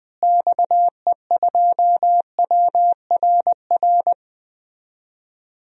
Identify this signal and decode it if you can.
Morse code, heard as a keyed tone, about 20 wpm, 700 Hz. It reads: XE2WRR
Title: DXCC-100 Call Signs - 6 Characters - Repeated 2x - 20wpm